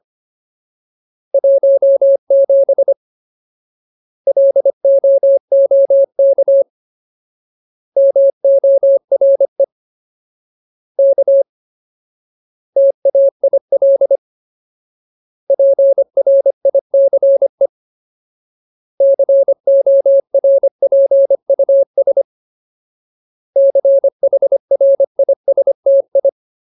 Morse code, heard as a keyed tone, about 25 wpm, 550 Hz. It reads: 17 LOOK MORE K TAIL PRICE CORPUS CHRISTI
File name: Ultimate Word List / 25WPM Ultimate